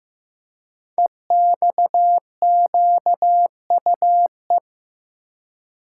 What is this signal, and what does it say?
Morse code, heard as a keyed tone, about 15 wpm, 700 Hz. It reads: EXQUE
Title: Code Groups - 5 Letter:Numbers - 15wpm